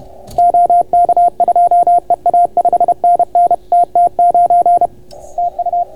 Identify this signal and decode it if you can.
Morse code, heard as a keyed tone, about 30 wpm, 690 Hz, with radio noise. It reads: OK2EA 5NN TT9 TU